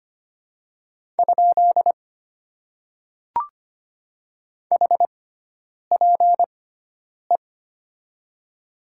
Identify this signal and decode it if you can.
Morse code, heard as a keyed tone, about 25 wpm, spaced as for 9 wpm, 700 Hz. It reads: ? HPE
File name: Top 90 QSO Elements - 6x Character Spacing - 25wpm